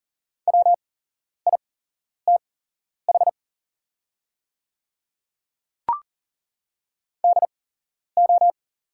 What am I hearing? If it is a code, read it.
Morse code, heard as a keyed tone, about 39 wpm, spaced as for 11 wpm, 690 Hz. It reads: WITH DO
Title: 8x Character Spacing - Top 300 Words - 40wpm